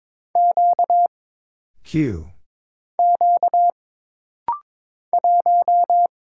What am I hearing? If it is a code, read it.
Morse code, heard as a keyed tone, about 22 wpm, 700 Hz. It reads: Q Q 1
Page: This rapid-fire timing is geared to push your ICR (Instant Character Recognition) skills.